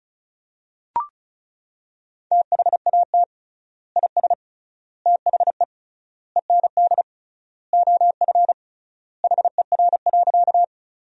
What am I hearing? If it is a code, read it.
Morse code, heard as a keyed tone, about 35 wpm, 700 Hz. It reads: THAT IS THE END OF HER.
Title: 3x Word-Spacing - Sentences from Top 200 Words - 35wpm